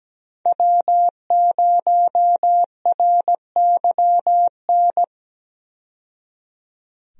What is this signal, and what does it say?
Morse code, heard as a keyed tone, about 17 wpm, 700 Hz. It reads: W0RYN
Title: USA Call Signs - 5 Characters - 17wpm